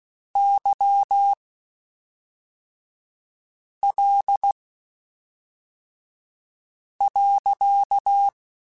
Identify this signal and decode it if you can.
Morse code, heard as a keyed tone, about 16 wpm, spaced as for 3 wpm, 800 Hz. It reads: YL.
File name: (spaced) / QSOs Spaced-mp3s / WA7QOX 2 @5 sp